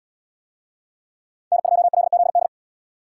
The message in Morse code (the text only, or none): N2LZD